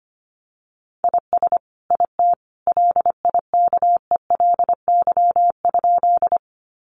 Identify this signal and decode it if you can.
Morse code, heard as a keyed tone, about 25 wpm, 700 Hz. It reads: IS IT LIKELY?
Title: Sentences from Top 600 Words 25wpm